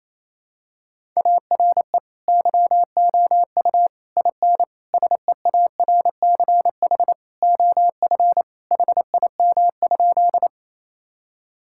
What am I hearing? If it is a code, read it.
Morse code, heard as a keyed tone, about 27 wpm, 700 Hz. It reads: ARE YOU IN SEARCH OF HIM?
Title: Sentences from Top 1000 Words - Review Part 5 - 28wpm